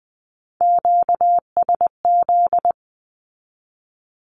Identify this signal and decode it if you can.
Morse code, heard as a keyed tone, about 20 wpm, 700 Hz. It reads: QSZ
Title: Q-Codes with Spoken Meaning - 20wpm